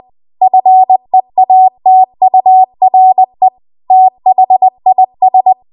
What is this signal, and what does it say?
Morse code, heard as a keyed tone, about 20 wpm, 750 Hz. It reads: FEATURE THIS
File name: Archive / 20 WPM / 20WPM